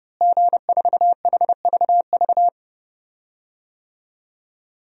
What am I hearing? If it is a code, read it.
Morse code, heard as a keyed tone, about 30 wpm, 700 Hz. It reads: G4HVV
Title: DXCC-100 Call Signs - 5 Characters - Repeated 3x - 30wpm